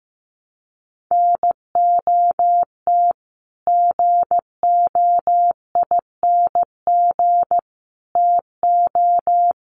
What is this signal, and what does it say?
Morse code, heard as a keyed tone, about 15 wpm, 700 Hz. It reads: NOT GOING TO
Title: Sets of 5 Words 15wpm